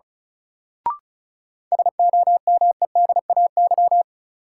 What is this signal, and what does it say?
Morse code, heard as a keyed tone, about 34 wpm, 700 Hz. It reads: SOMEDAY
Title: Sentences from Top 1600 Words - New Word Drill Alt - 35wpm